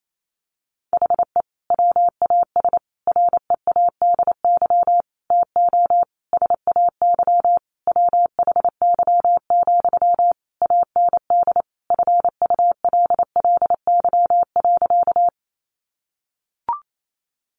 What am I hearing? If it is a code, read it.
Morse code, heard as a keyed tone, about 28 wpm, 700 Hz. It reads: HE WAS READY TO SAY WHY, AND FULLY.
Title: Sentences from Top 1300 Words 28wpm